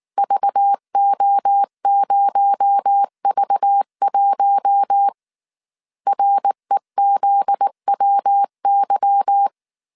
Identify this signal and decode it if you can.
Morse code, heard as a keyed tone, about 19 wpm, 800 Hz. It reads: VO0V1 REZWY